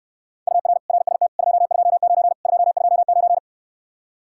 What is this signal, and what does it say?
Morse code, heard as a keyed tone, about 69 wpm, 700 Hz. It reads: UR RST 229 229